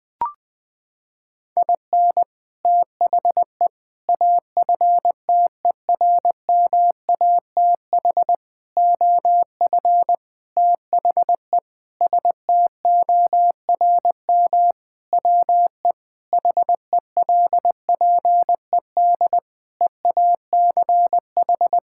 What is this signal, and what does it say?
Morse code, heard as a keyed tone, about 20 wpm, 700 Hz. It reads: IN THE AFTERMATH OF THE STORM WE HELPED EACH
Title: Sentences with 5-Word Phrases - 20wpm